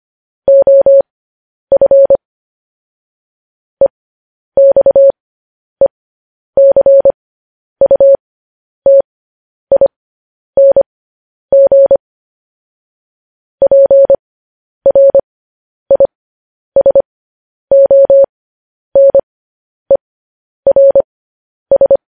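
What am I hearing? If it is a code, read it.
Morse code, heard as a keyed tone, about 25 wpm, spaced as for 10 wpm, 550 Hz. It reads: OF EXECUTING PRISONERS
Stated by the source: Morse code news headlines practice transmission.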